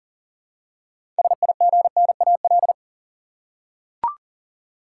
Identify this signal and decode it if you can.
Morse code, heard as a keyed tone, about 40 wpm, 700 Hz. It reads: SIGNAL
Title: Six Letter Words - 40wpm